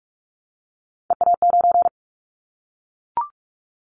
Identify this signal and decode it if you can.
Morse code, heard as a keyed tone, about 44 wpm, 700 Hz. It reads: EA9
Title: Callsign Prefixes for Top 100 DXCC - 45wpm